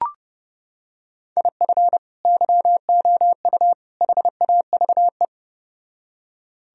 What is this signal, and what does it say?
Morse code, heard as a keyed tone, about 29 wpm, 700 Hz. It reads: IF YOU HAVE